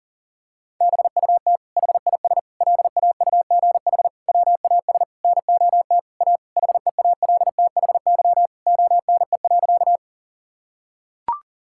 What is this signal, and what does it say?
Morse code, heard as a keyed tone, about 37 wpm, 700 Hz. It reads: BUT HIS LAUGH WAS NOT A HEALTHY ONE.
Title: Sentences from Top 1000 Words - Review Part 5 - 40wpm